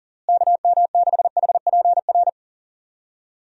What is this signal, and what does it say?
Morse code, heard as a keyed tone, about 40 wpm, 700 Hz. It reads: KM6HPR
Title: USA Call Signs - 6 Characters - Repeated 3x - 40wpm